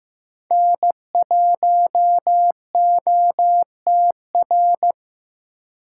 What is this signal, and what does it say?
Morse code, heard as a keyed tone, about 18 wpm, 700 Hz. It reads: N1OTR
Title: Speed Racing - USA 1x3 Call Signs 15wpm